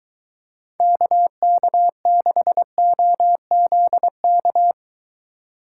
Morse code, heard as a keyed tone, about 23 wpm, 700 Hz. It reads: KK6OZK